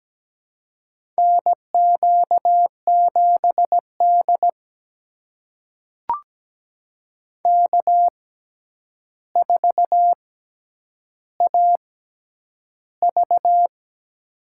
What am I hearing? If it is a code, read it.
Morse code, heard as a keyed tone, about 17 wpm, spaced as for 9 wpm, 700 Hz. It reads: NQ7D K4AV
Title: USA Call Signs - 4 Characters - 6x Character Spacing - 17wpm